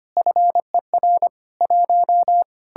Morse code, heard as a keyed tone, about 32 wpm, 700 Hz. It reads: FER 1